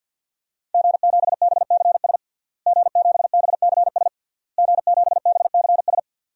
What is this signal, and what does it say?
Morse code, heard as a keyed tone, about 52 wpm, 700 Hz. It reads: G7BCS G7BCS G7BCS
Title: DXCC-100 Call Signs - 5 Characters - Repeated 3x - 50wpm